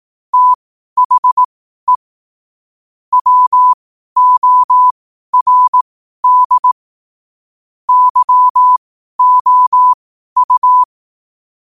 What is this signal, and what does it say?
Morse code, heard as a keyed tone, about 18 wpm, spaced as for 12 wpm, 1000 Hz. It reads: THE WORD YOU
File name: qotdfiles / 12wpm